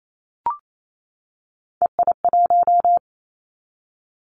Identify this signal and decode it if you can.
Morse code, heard as a keyed tone, about 28 wpm, 700 Hz. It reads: EI1
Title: Callsign Prefixes for Top 50 DXCC - 28wpm